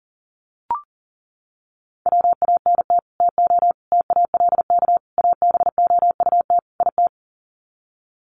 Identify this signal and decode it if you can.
Morse code, heard as a keyed tone, about 40 wpm, 700 Hz. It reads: WANT TO TALK ABOUT IT
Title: Sets of 5 Words Encore 40wpm